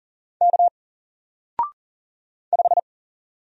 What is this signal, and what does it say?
Morse code, heard as a keyed tone, about 40 wpm, 700 Hz. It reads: K 5
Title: Single Letter-Number 40wpm